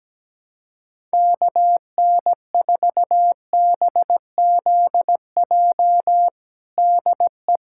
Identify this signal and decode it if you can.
Morse code, heard as a keyed tone, about 17 wpm, 700 Hz. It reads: KN4BZJ DE
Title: USA Callsign DE Callsign - 6 Characters - 17wpm